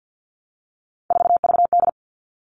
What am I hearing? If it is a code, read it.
Morse code, heard as a keyed tone, about 50 wpm, 700 Hz. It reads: 4VD
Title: All Cut Numbers - 3-Digit 50wpm